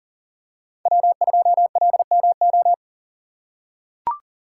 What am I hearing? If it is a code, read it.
Morse code, heard as a keyed tone, about 37 wpm, 700 Hz. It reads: W2LMO
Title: USA Call Signs - 5 Characters - 3x Character Spacing - 40wpm